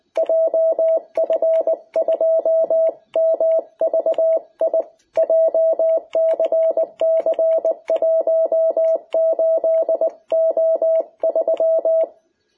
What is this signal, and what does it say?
Morse code, heard as a keyed tone, about 20 wpm, 660 Hz. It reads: JF2MVI JCC18O3
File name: jh8cbh / 2021cw